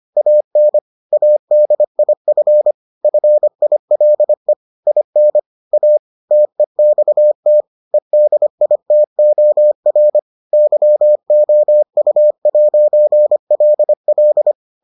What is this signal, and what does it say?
Morse code, heard as a keyed tone, about 25 wpm, 600 Hz. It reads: AN ADIF FILE IN A TEXT EDITOR YOU'LL